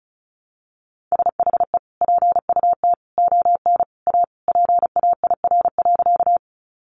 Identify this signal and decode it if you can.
Morse code, heard as a keyed tone, about 34 wpm, 700 Hz. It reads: SHE PUT ON A PAIR.
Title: Sentences from Top 1200 Words 35wpm